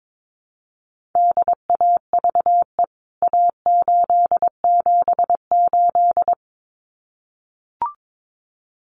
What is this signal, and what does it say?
Morse code, heard as a keyed tone, about 22 wpm, 700 Hz. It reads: DAVE A878
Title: CWT Exchange 22wpm